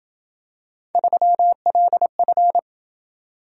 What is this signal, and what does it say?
Morse code, heard as a keyed tone, about 28 wpm, 700 Hz. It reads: 3LF